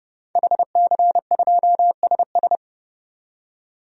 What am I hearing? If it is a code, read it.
Morse code, heard as a keyed tone, about 31 wpm, 700 Hz. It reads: HC2SS
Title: DXCC-100 Call Signs - 5 Characters - 30wpm